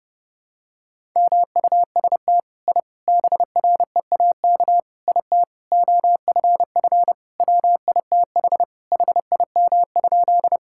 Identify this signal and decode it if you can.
Morse code, heard as a keyed tone, about 30 wpm, 700 Hz. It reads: MUST I BREAK IT OFF WITH HIM?
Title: Sentences from Top 1000 Words - Review Part 6 - 30wpm